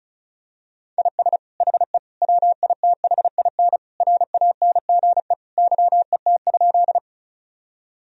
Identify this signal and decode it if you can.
Morse code, heard as a keyed tone, about 36 wpm, 700 Hz. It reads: IS HE WITHIN RANGE YET?
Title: Sentences from Top 1000 Words - Review Part 4 - 35wpm